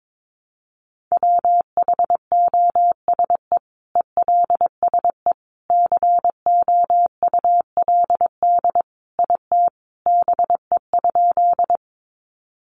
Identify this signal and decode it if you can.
Morse code, heard as a keyed tone, about 22 wpm, 700 Hz. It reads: WHOSE ELSE COULD IT BE?
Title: Sentences from Top 1200 Words 22wpm